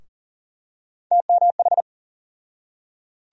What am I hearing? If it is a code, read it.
Morse code, heard as a keyed tone, about 39 wpm, 700 Hz. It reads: TMH
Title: USA Call Sign Postfixes - 3 Letters 40wpm